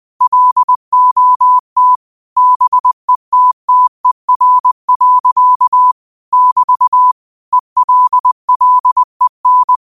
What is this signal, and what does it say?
Morse code, heard as a keyed tone, about 20 wpm, 1000 Hz. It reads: LOT BETTER. <BT> ELLEN